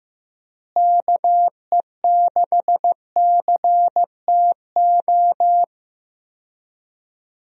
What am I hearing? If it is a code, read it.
Morse code, heard as a keyed tone, about 15 wpm, 700 Hz. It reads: KE6CTO
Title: USA Call Signs - 6 Characters - Repeated 2x - 15wpm